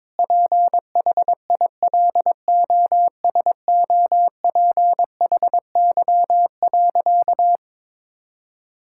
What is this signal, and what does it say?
Morse code, heard as a keyed tone, about 21 wpm, 700 Hz. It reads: PHILOSOPHY.
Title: Sentences from Top 1000 Words - Review Part 6 - 22wpm